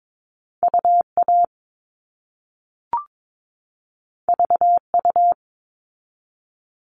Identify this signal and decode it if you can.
Morse code, heard as a keyed tone, about 22 wpm, 700 Hz. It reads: UA VU
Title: All Cut Numbers - 2-Digit 22wpm